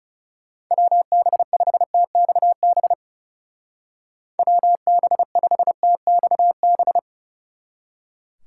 Morse code, heard as a keyed tone, about 37 wpm, 700 Hz. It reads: WB5TXB WB5TXB
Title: Speed Racing - USA 2x3 Call Signs 30wpm